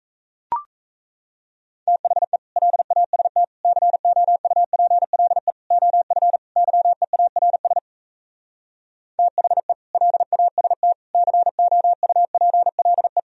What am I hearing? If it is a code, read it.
Morse code, heard as a keyed tone, about 38 wpm, 700 Hz. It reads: THE LAST COUPLE OF YEARS THE LAST COUPLE